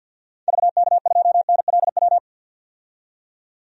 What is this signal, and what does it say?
Morse code, heard as a keyed tone, about 50 wpm, 700 Hz. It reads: VK2NLW
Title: DXCC-100 Call Signs - 6 Characters - 50wpm